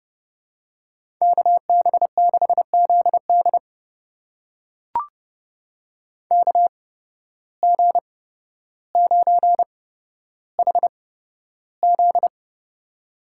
Morse code, heard as a keyed tone, about 31 wpm, spaced as for 13 wpm, 700 Hz. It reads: KB6ZD KG9HZ
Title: USA Call Signs - 5 Characters - 8x Character Spacing - 30wpm